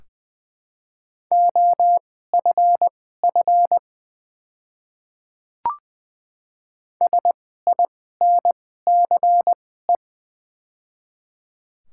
Morse code, heard as a keyed tone, about 20 wpm, spaced as for 15 wpm, 700 Hz. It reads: OFF SINCE
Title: 2x Character Spacing - Top 500 Words - 20wpm